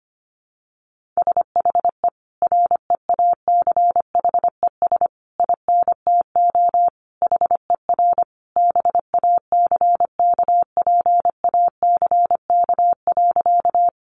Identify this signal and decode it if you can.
Morse code, heard as a keyed tone, about 25 wpm, 700 Hz. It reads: SHE REACHES INTO HER BACKPACK.
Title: Additional Easy Phrases 25wpm